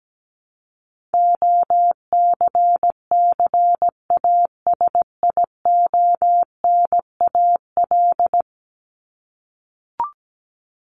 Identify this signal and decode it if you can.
Morse code, heard as a keyed tone, about 17 wpm, 700 Hz. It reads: OCCASIONAL
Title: LY Suffix - Top 100 Words with Base Words 17wpm